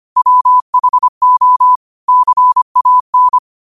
Morse code, heard as a keyed tone, about 25 wpm, 1000 Hz. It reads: WHO CAN
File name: qotdfiles / 25wpm